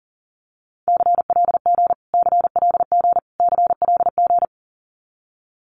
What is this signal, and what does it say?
Morse code, heard as a keyed tone, about 40 wpm, 700 Hz. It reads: CLG CLG CLG
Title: Abbreviations Repeated 3x - 40wpm